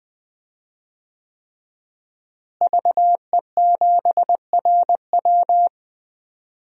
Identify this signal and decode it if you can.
Morse code, heard as a keyed tone, about 20 wpm, 700 Hz. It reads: VE7RW
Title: DXCC-100 Call Signs - 5 Characters - Repeated 3x - 20wpm